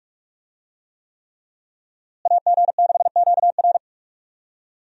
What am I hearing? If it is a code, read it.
Morse code, heard as a keyed tone, about 46 wpm, 700 Hz. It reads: AG6QR
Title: USA Call Signs - 5 Characters - Repeated 3x - 45wpm